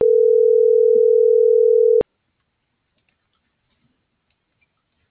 Ringback tone (North America), 440 + 480 Hz.